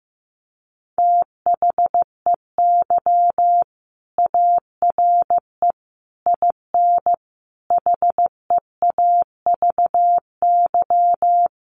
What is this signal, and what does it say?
Morse code, heard as a keyed tone, about 15 wpm, 700 Hz. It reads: THEY ARE IN HEAVY
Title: Sentences from Top 1300 Words 15wpm